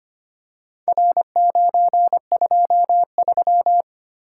Morse code, heard as a keyed tone, about 25 wpm, 700 Hz. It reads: R923
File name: practice / 25wpm / Four Letter-Number Combinations - 25wpm